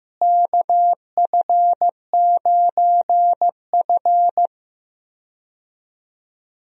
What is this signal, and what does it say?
Morse code, heard as a keyed tone, about 15 wpm, 700 Hz. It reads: KF9F
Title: USA Call Signs - 4 Characters - Repeated 2x - 15wpm